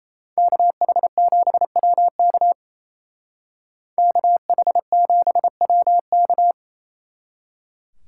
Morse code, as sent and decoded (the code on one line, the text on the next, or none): -.- .... --... .-- -.- / -.- .... --... .-- -.-
KH7WK KH7WK